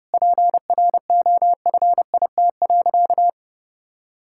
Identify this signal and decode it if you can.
Morse code, heard as a keyed tone, about 30 wpm, 700 Hz. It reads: PROFIT.
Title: Sentences from Top 1000 Words - Review Part 5 - 30wpm